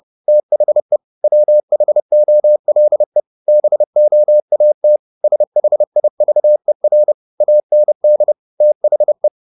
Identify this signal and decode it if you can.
Morse code, heard as a keyed tone, about 30 wpm, 600 Hz. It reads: THE WHOLE BOAT SHIVER AND THE
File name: the old man and the sea / the old man and the sea 1 30wpm